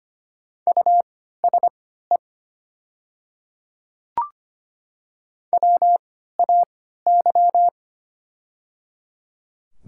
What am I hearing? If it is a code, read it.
Morse code, heard as a keyed tone, about 25 wpm, spaced as for 14 wpm, 700 Hz. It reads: USE WAY
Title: 3x Character Spacing - Top 100 Words - 25wpm